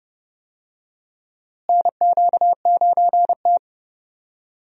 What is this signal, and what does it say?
Morse code, heard as a keyed tone, about 32 wpm, 700 Hz. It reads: NQ9T
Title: USA Call Signs - 4 Characters - Repeated 2x - 30wpm